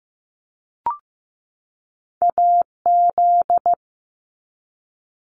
Morse code, heard as a keyed tone, about 15 wpm, 700 Hz. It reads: AZ
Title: ARRL Sections 15wpm